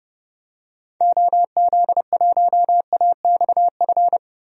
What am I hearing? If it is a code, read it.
Morse code, heard as a keyed tone, about 30 wpm, 700 Hz. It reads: OZ1AXF
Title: DXCC-100 Call Signs - 6 Characters - 30wpm